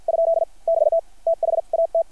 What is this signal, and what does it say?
Morse code, heard as a keyed tone, about 50 wpm, spaced as for 42 wpm, 660 Hz. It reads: ? <BT> THAT
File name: archived / CRSnet